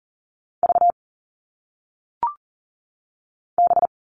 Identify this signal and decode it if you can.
Morse code, heard as a keyed tone, about 40 wpm, 700 Hz. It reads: V B
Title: All Cut Numbers - 1-Digit 40wpm